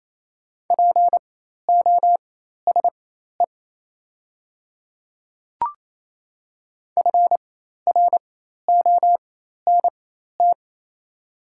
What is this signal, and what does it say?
Morse code, heard as a keyed tone, about 27 wpm, spaced as for 13 wpm, 690 Hz. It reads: POSE FRONT
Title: 4x Character Spacing - Top 500 Words - 28wpm